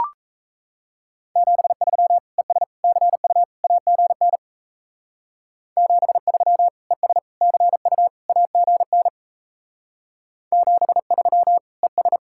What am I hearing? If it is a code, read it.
Morse code, heard as a keyed tone, about 34 wpm, 700 Hz. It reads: 73 ES CU AGN 73 ES CU AGN 73 ES
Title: Speed-Racing Common QSO Phrases - 28wpm